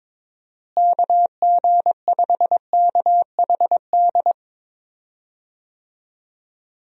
Morse code, heard as a keyed tone, about 22 wpm, 700 Hz. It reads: KG5KHD
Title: USA Call Signs - 6 Characters - Repeated 3x - 22wpm